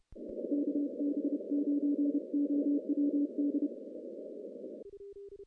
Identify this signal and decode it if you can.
Morse code, heard as a keyed tone, about 30 wpm, 290 Hz. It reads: KB9OWD
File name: contests / CW-SS-k5tr- -0000